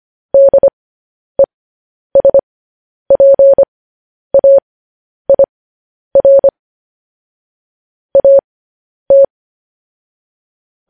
Morse code, heard as a keyed tone, about 25 wpm, spaced as for 10 wpm, 550 Hz. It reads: DESPAIR AT
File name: podcast / News headlines-1x-25x10